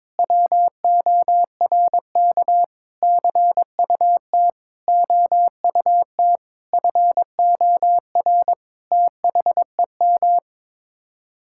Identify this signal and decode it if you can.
Morse code, heard as a keyed tone, about 22 wpm, 700 Hz. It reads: WORK CUT OUT FOR THEM
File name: practice / 22wpm / Sets of 7 Words - 22wpm